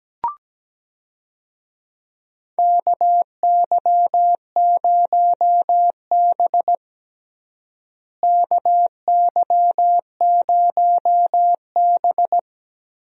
Morse code, heard as a keyed tone, about 17 wpm, 700 Hz. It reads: KY0B KY0B
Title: USA Call Signs - 4 Characters - Repeated 3x - 17wpm